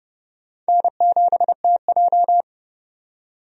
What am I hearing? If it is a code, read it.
Morse code, heard as a keyed tone, about 30 wpm, 700 Hz. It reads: N7TJ
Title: USA Call Signs - 4 Characters - 30wpm